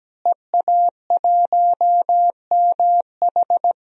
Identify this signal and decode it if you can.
Morse code, heard as a keyed tone, about 17 wpm, 700 Hz. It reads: EA1MH
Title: DXCC-100 Call Signs - 5 Characters - Repeated 3x - 17wpm